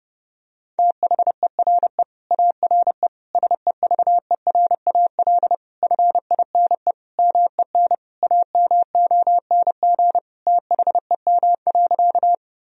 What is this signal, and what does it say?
Morse code, heard as a keyed tone, about 30 wpm, 700 Hz. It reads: THERE ARE SEVERAL FINE MEN AMONG THEM.
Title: Sentences from Top 1000 Words - Review Part 2 - 30wpm